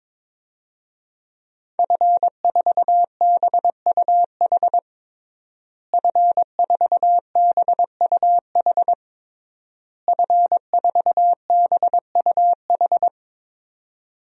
Morse code, heard as a keyed tone, about 22 wpm, 700 Hz. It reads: F4BUH F4BUH F4BUH
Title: DXCC-100 Call Signs - 5 Characters - Repeated 3x - 22wpm